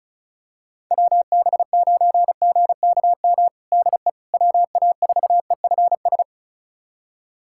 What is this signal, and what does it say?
Morse code, heard as a keyed tone, about 34 wpm, 700 Hz. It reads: WB9GKM DE WA4EFS
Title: USA Callsign DE Callsign - 6 Characters - 35wpm